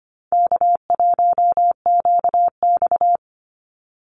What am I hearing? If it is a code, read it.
Morse code, heard as a keyed tone, about 24 wpm, 700 Hz. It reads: K1QX
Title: Easy Call Signs - 25wpm